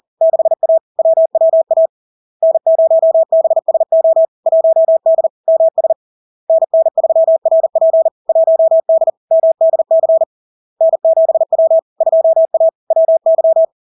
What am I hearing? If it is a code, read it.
Morse code, heard as a keyed tone, about 40 wpm, 650 Hz. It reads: 6A WWA N0BSO 1D MS NN3RP 1D MDC N7W 2A WY
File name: LARK-FD-2022-Log-in-CW-40wpm-650-Hz